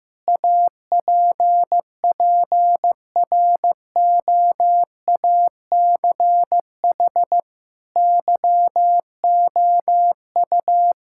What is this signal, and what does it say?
Morse code, heard as a keyed tone, about 15 wpm, 700 Hz. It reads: APPROACH YOU
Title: Sentences from Top 1000 Words - Review Part 6 - 15wpm